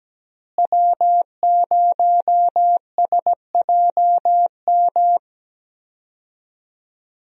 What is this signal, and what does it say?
Morse code, heard as a keyed tone, about 17 wpm, 700 Hz. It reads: W0SJM
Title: USA Call Signs - 5 Characters - Repeated 3x - 17wpm